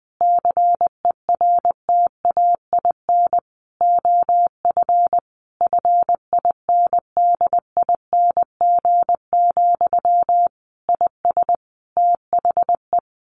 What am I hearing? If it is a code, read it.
Morse code, heard as a keyed tone, about 19 wpm, 700 Hz. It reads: CERTAIN OF FINDING, IS THE
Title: Sentences from Top 1100 Words 20wpm